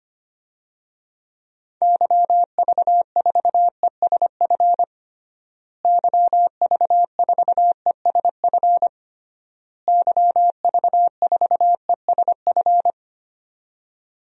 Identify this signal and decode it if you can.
Morse code, heard as a keyed tone, about 25 wpm, 700 Hz. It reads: YV4ESF YV4ESF YV4ESF
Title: DXCC-100 Call Signs - 6 Characters - Repeated 3x - 25wpm